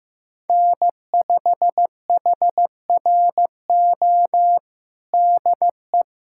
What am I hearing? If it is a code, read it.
Morse code, heard as a keyed tone, about 15 wpm, 700 Hz. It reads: N5HRO DE
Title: USA Callsign DE Callsign 15wpm